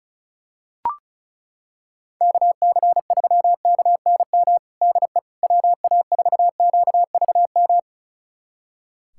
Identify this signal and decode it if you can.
Morse code, heard as a keyed tone, about 36 wpm, 700 Hz. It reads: KC3KNM DE WA4QVM
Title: USA Callsign DE Callsign - 6 Characters - 35wpm